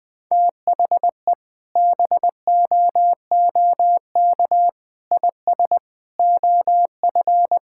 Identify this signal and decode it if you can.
Morse code, heard as a keyed tone, about 20 wpm, 700 Hz. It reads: THE BOOK IS OF
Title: Sentences with 2-Word Phrases - 20wpm